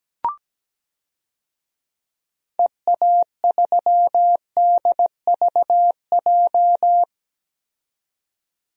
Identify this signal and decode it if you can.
Morse code, heard as a keyed tone, about 17 wpm, 700 Hz. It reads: EA3DVJ
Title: DXCC-100 Call Signs - 6 Characters - 17wpm